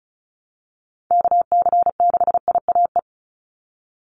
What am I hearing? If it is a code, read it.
Morse code, heard as a keyed tone, about 34 wpm, 700 Hz. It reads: KC6IAE
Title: USA Call Signs 35wpm